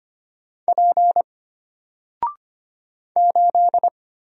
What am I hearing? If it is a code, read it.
Morse code, heard as a keyed tone, about 25 wpm, 700 Hz. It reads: P 8
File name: Single-Letter-Number-25wpm